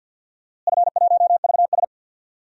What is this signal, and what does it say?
Morse code, heard as a keyed tone, about 48 wpm, 700 Hz. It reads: F1VS
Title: DXCC-100 Call Signs - 4 Characters - Repeated 2x - 50wpm